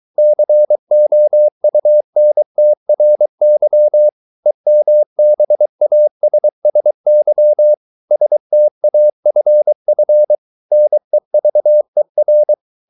Morse code, heard as a keyed tone, about 23 wpm, 600 Hz. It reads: COUNTRY EMBASSY STAFF NEVER
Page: Romanen Never af Ken Follet på engelsk.